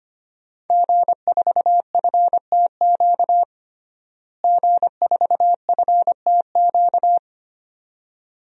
Morse code, heard as a keyed tone, about 25 wpm, 700 Hz. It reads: G4FTQ G4FTQ
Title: DXCC-100 Call Signs - 5 Characters - Repeated 2x - 25wpm